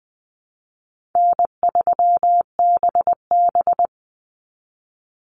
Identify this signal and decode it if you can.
Morse code, heard as a keyed tone, about 20 wpm, 700 Hz. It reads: N3BB
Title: Easy Call Signs Repeated 2x - 20wpm